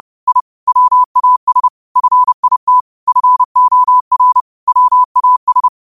Morse code, heard as a keyed tone, about 30 wpm, 1000 Hz. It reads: I WAS FIT FOR WAS
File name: qotdfiles / 30wpm